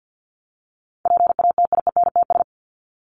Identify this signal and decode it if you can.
Morse code, heard as a keyed tone, about 50 wpm, 700 Hz. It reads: PATIENTS
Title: ES:S Suffix - Top 100 Words with Base Words 50wpm